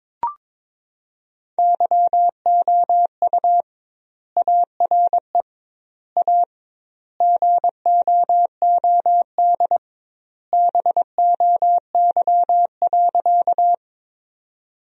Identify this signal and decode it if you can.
Morse code, heard as a keyed tone, about 21 wpm, 700 Hz. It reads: YOU ARE A GOOD BOY.
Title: 2x Word-Spacing - Sentences from Top 200 Words - 22wpm